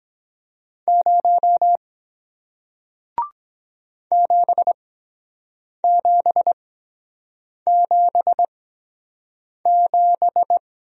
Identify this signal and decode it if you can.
Morse code, heard as a keyed tone, about 23 wpm, spaced as for 7 wpm, 700 Hz. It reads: 0 7777